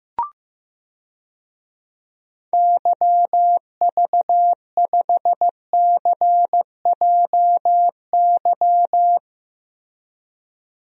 Morse code, heard as a keyed tone, about 15 wpm, 700 Hz. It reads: YV5CJY
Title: DXCC-100 Call Signs - 6 Characters - Repeated 2x - 15wpm